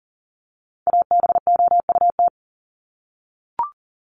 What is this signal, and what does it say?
Morse code, heard as a keyed tone, about 40 wpm, 700 Hz. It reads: ABOUT
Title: Top 500 Words 40wpm